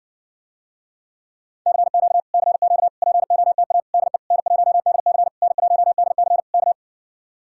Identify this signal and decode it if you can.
Morse code, heard as a keyed tone, about 60 wpm, 700 Hz. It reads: CQ CQ POTA DE N1DC N1DC K